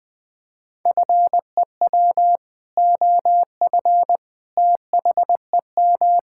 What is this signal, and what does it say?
Morse code, heard as a keyed tone, about 20 wpm, 700 Hz. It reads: FEW OF THEM
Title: Sentences with 2-Word Phrases - 20wpm